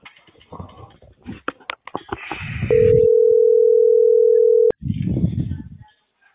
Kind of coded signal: telephone call progress tone